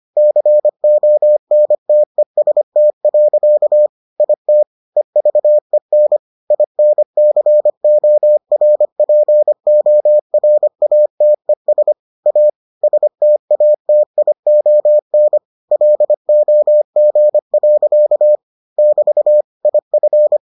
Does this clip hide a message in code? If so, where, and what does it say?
Yes: Morse code from 0.2 to 20.4 s, heard as a keyed tone, about 25 wpm, 600 Hz. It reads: CONTEST. IT EVEN INCORPORATES A STATION LOG. <BT> IF